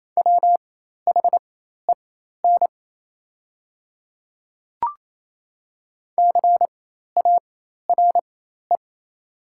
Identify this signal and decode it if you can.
Morse code, heard as a keyed tone, about 28 wpm, spaced as for 13 wpm, 700 Hz. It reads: WHEN CARE
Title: 4x Character Spacing - Top 300 Words - 28wpm